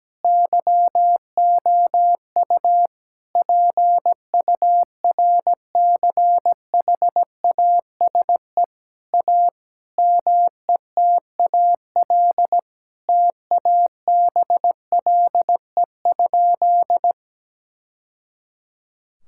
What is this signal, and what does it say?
Morse code, heard as a keyed tone, about 17 wpm, 700 Hz. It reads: YOU PURCHASE A METAL TABLE?
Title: Sentences from Top 1000 Words - Review Part 6 - 17wpm